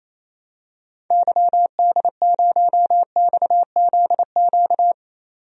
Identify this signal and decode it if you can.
Morse code, heard as a keyed tone, about 28 wpm, 700 Hz. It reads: YD0XZQ
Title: DXCC-100 Call Signs - 6 Characters - 28wpm